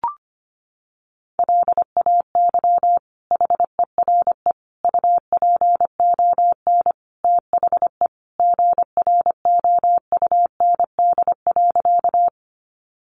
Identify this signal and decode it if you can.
Morse code, heard as a keyed tone, about 25 wpm, 700 Hz. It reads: LAY HERE UPON THE GROUND.
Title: Sentences from Top 1100 Words 25wpm